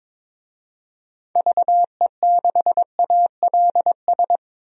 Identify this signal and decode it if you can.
Morse code, heard as a keyed tone, about 22 wpm, 700 Hz. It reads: VE6ALS
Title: DXCC-100 Call Signs - 6 Characters - 22wpm